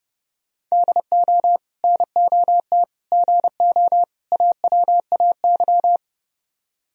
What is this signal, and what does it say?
Morse code, heard as a keyed tone, about 39 wpm, 700 Hz. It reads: DO NOT GO AWAY